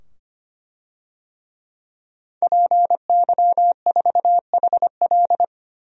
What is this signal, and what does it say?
Morse code, heard as a keyed tone, about 25 wpm, 700 Hz. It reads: PY4HL
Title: DXCC-100 Call Signs - 5 Characters - 25wpm